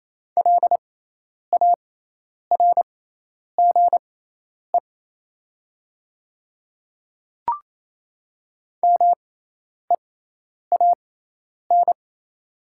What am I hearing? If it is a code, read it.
Morse code, heard as a keyed tone, about 28 wpm, spaced as for 10 wpm, 700 Hz. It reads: LARGE MEAN
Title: 6x Character Spacing - Top 200 Words - 28wpm